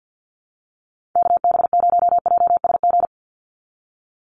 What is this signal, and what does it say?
Morse code, heard as a keyed tone, about 50 wpm, 700 Hz. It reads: KB0JSG
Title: USA Call Signs 50wpm